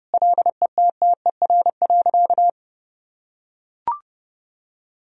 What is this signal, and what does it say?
Morse code, heard as a keyed tone, about 32 wpm, 700 Hz. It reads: LETTER.
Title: Sentences from Top 1000 Words - Review Part 4 - 30wpm